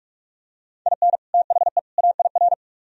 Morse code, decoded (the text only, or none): IN THE AIR